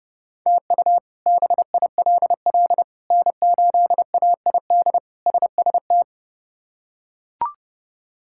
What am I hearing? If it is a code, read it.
Morse code, heard as a keyed tone, about 30 wpm, 700 Hz. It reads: TU BILL N8AID SST
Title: K1USN SST QSO Practice - 30wpm